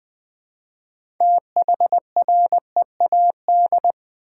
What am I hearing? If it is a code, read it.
Morse code, heard as a keyed tone, about 20 wpm, 700 Hz. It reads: THREAD
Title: Easy Words with Hints - 20wpm